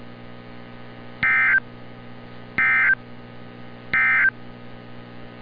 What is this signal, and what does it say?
Emergency Alert System (SAME) tones, heard as harsh buzzing bursts. It is heard from 1.1 to 4.4 s.
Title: NOAA Weather Radio - Altoona, PA